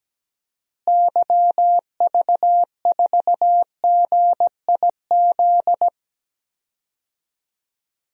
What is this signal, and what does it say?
Morse code, heard as a keyed tone, about 17 wpm, 700 Hz. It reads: YV4GIZ
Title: DXCC-100 Call Signs - 6 Characters - Repeated 2x - 17wpm